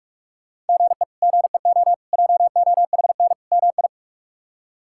Morse code, heard as a keyed tone, about 44 wpm, 700 Hz. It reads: GE GEO JOHN MI
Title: K1USN SST QSO Practice - 45wpm